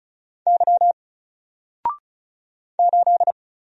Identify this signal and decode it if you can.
Morse code, heard as a keyed tone, about 36 wpm, 700 Hz. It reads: Y 8
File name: Single-Letter-Number-35wpm